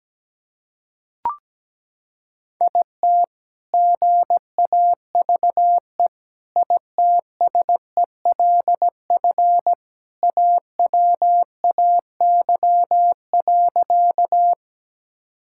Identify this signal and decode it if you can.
Morse code, heard as a keyed tone, about 17 wpm, 700 Hz. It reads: IT GAVE ITSELF AWAY.
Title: Sentences from Top 1000 Words - Review Part 5 - 17wpm